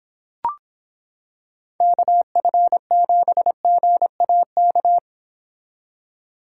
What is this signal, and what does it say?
Morse code, heard as a keyed tone, about 26 wpm, 700 Hz. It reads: KF7GAK